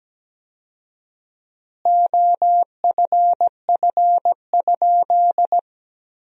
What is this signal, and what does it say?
Morse code, heard as a keyed tone, about 17 wpm, 700 Hz. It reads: OFF?
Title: 4x Word-Spacing - Sentences from Top 500 Words - 17wpm